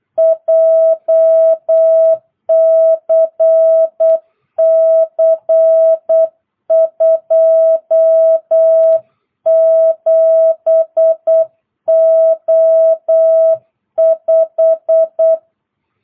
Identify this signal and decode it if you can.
Morse code, heard as a keyed tone, about 8 wpm, 660 Hz. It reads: JCC 27O5